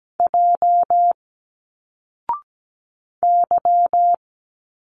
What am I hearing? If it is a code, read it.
Morse code, heard as a keyed tone, about 17 wpm, 700 Hz. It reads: J Y
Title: Single Letters 17wpm